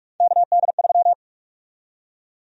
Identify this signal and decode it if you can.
Morse code, heard as a keyed tone, about 45 wpm, 700 Hz. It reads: KD3
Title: USA Call Sign Prefixes - 45wpm